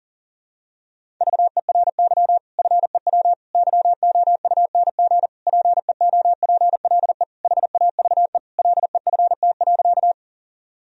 Morse code, heard as a keyed tone, about 40 wpm, 700 Hz. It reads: VERY FEW YOUNG PEOPLE HAVE LEFT.
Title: Sentences from Top 1000 Words - Review Part 1 - 40wpm